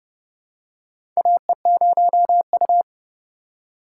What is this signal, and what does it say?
Morse code, heard as a keyed tone, about 31 wpm, 700 Hz. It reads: AE0U
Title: USA Call Signs - 4 Characters - 4x Character Spacing - 30wpm